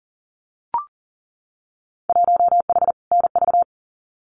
Morse code, heard as a keyed tone, about 40 wpm, 700 Hz. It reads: 1H NV
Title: Winter Field Day Exchanges 40wpm